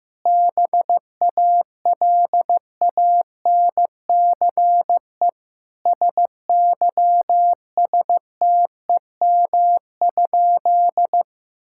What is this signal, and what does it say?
Morse code, heard as a keyed tone, about 15 wpm, 700 Hz. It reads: BALANCE SYSTEM?
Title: Sentences from Top 1000 Words - Review Part 6 - 15wpm